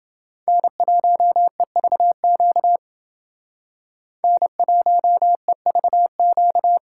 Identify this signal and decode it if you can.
Morse code, heard as a keyed tone, about 29 wpm, 700 Hz. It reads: N1EVQ N1EVQ